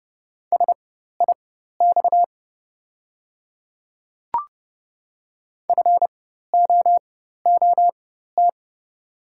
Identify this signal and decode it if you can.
Morse code, heard as a keyed tone, about 32 wpm, spaced as for 14 wpm, 700 Hz. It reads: SIX FOOT
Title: 4x Character Spacing - Top 500 Words - 30wpm